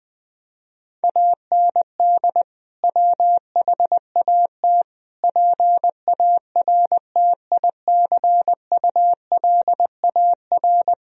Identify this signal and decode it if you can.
Morse code, heard as a keyed tone, about 21 wpm, 700 Hz. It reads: AND WHAT PARTICULAR
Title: Sentences from Top 1000 Words - Review Part 4 - 20wpm